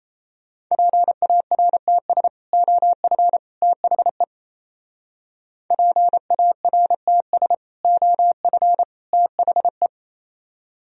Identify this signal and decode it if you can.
Morse code, heard as a keyed tone, about 33 wpm, 700 Hz. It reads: PARTS OF THE PARTS OF THE
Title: Speed-Racing - Sets of 3 Words 28wpm